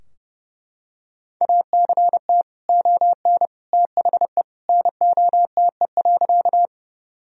Morse code, heard as a keyed tone, about 30 wpm, 700 Hz. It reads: ACT ON THE NOTE.
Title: Sentences from Top 1000 Words - Review Part 2 - 30wpm